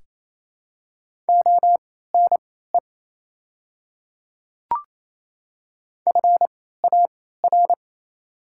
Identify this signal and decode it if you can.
Morse code, heard as a keyed tone, about 28 wpm, spaced as for 16 wpm, 700 Hz. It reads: ONE FAR
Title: 3x Character Spacing - Top 300 Words - 28wpm